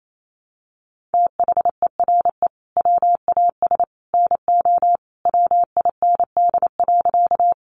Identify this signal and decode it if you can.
Morse code, heard as a keyed tone, about 28 wpm, 700 Hz. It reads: THERE WAS NO WIND.
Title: Sentences from Top 400 Words 28wpm